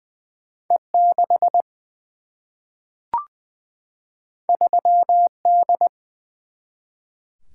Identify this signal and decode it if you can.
Morse code, heard as a keyed tone, about 20 wpm, 700 Hz. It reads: E 6 3 D
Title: Single-Letter Single-Number 20wpm